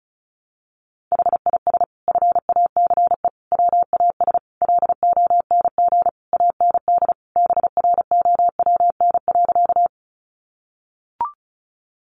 Morse code, heard as a keyed tone, about 34 wpm, 700 Hz. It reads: HIS FACE WAS LONG AND BROWN.
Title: Sentences from Top 1200 Words 35wpm